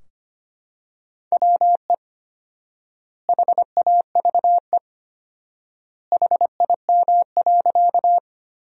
Morse code, heard as a keyed tone, about 25 wpm, 700 Hz. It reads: WE HAVE HIM.
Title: 4x Word-Spacing - Sentences from Top 100 Words - 25wpm